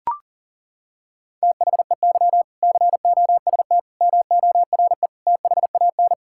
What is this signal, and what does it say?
Morse code, heard as a keyed tone, about 37 wpm, 700 Hz. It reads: THEY COST MORE THAN
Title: Sentences from Top 1000 Words - Review Part 5 - 40wpm